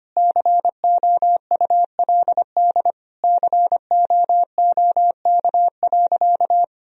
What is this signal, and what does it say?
Morse code, heard as a keyed tone, about 25 wpm, 700 Hz. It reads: COULD COOK.
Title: Sentences from Top 1000 Words - Review Part 4 - 25wpm